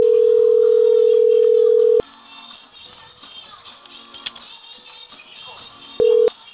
Ringback tone (North America), 440 + 480 Hz.
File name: CENTRAL CAEA / out agcaea02 - .23048